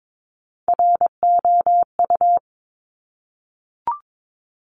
Morse code, heard as a keyed tone, about 22 wpm, 700 Hz. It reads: ROU
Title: Top Three Letter Combinations in Words 22wpm